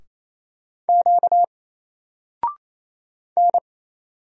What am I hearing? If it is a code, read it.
Morse code, heard as a keyed tone, about 27 wpm, 700 Hz. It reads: Q N
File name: Single-Letter-Number-28wpm